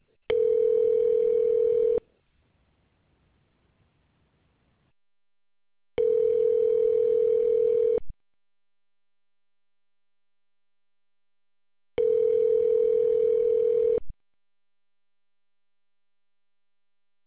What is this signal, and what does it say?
Ringback tone (North America), 440 + 480 Hz.